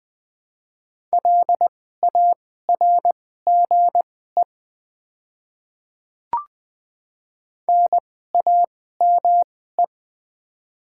Morse code, heard as a keyed tone, about 20 wpm, spaced as for 14 wpm, 700 Hz. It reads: LARGE NAME
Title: 2x Character Spacing - Top 200 Words - 20wpm